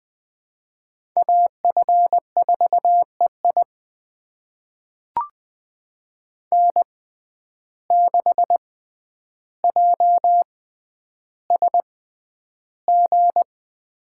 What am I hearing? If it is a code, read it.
Morse code, heard as a keyed tone, about 20 wpm, spaced as for 10 wpm, 700 Hz. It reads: AF4EI N6JSG
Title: USA Call Signs - 5 Characters - 6x Character Spacing - 20wpm